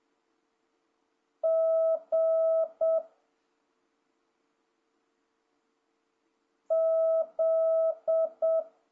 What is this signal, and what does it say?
Morse code, heard as a keyed tone, about 7 wpm, 650 Hz. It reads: G Z
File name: jh8cbh / 2026cw